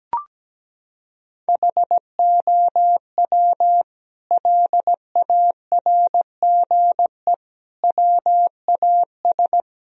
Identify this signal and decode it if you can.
Morse code, heard as a keyed tone, about 17 wpm, 700 Hz. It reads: HOW LARGE WAS
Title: Sentences from Top 1000 Words - Review Part 4 - 17wpm